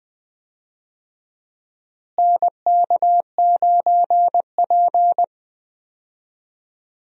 Morse code, heard as a keyed tone, about 20 wpm, 700 Hz. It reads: NK9P
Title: USA Call Signs - 4 Characters - Repeated 2x - 20wpm